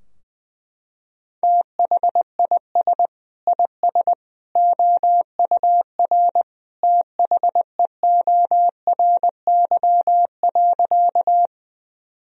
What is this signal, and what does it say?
Morse code, heard as a keyed tone, about 20 wpm, 700 Hz. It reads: THIS IS OUR THEORY.
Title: Sentences from Top 1000 Words - Review Part 3 - 20wpm